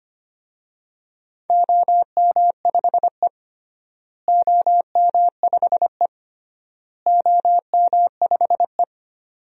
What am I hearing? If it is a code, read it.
Morse code, heard as a keyed tone, about 25 wpm, 700 Hz. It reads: OM5E OM5E OM5E
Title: DXCC-100 Call Signs - 4 Characters - Repeated 3x - 25wpm